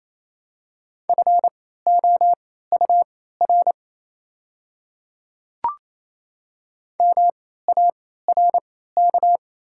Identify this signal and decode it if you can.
Morse code, heard as a keyed tone, about 28 wpm, spaced as for 16 wpm, 700 Hz. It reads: FOUR MARK
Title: 3x Character Spacing - Top 300 Words - 28wpm